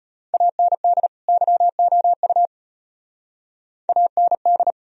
Morse code, heard as a keyed tone, about 32 wpm, 700 Hz. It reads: AND YOU AND